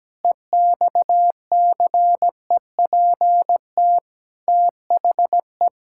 Morse code, heard as a keyed tone, about 17 wpm, 700 Hz. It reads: EXCEPT THE
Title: Sentences from Top 1600 Words 17wpm